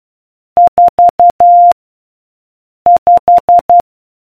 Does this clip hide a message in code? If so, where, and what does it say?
Yes: Morse code from 0.6 to 3.8 s, heard as a keyed tone, about 12 wpm, 700 Hz. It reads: 4 5